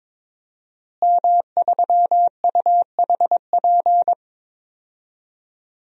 Morse code, heard as a keyed tone, about 22 wpm, 700 Hz. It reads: M3UHP
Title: DXCC-100 Call Signs - 5 Characters - Repeated 2x - 22wpm